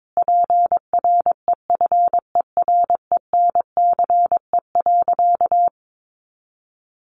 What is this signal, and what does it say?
Morse code, heard as a keyed tone, about 22 wpm, 700 Hz. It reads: PREFERENCE.
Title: Sentences from Top 1500 Words 22wpm